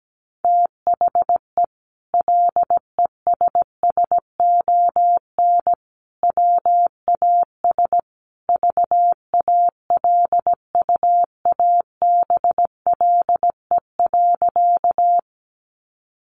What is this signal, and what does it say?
Morse code, heard as a keyed tone, about 17 wpm, 700 Hz. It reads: THE LESSON WAS VALUABLE.
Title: Sentences from Top 1200 Words 17wpm